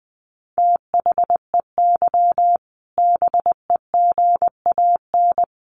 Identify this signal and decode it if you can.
Morse code, heard as a keyed tone, about 20 wpm, 700 Hz. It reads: THEY BEGAN
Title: A Sound Of Thunder 20wpm